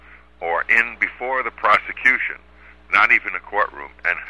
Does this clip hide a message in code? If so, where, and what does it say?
Yes: Emergency Alert System (SAME) tones from 0.6 to 3.4 s, heard as harsh buzzing bursts, with speech.